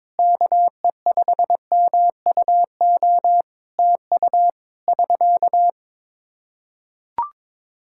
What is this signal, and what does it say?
Morse code, heard as a keyed tone, about 22 wpm, 700 Hz. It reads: KE5MUO TU <SK>
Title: Practice QSO - Exchange 4 - Ending - 22wpm